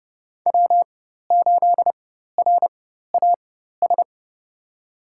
Morse code, heard as a keyed tone, about 30 wpm, spaced as for 14 wpm, 700 Hz. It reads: W8RAS
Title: USA Call Signs - 5 Characters - 4x Character Spacing - 30wpm